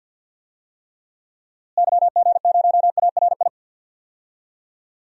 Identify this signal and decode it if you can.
Morse code, heard as a keyed tone, about 50 wpm, 700 Hz. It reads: YG0ARI
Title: DXCC-100 Call Signs - 6 Characters - Repeated 3x - 50wpm